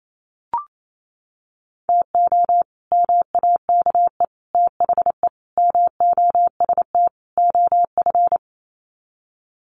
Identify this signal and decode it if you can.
Morse code, heard as a keyed tone, about 28 wpm, 700 Hz. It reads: TO MAKE THE MOST OF
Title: Sets of 5 Words Encore2 28wpm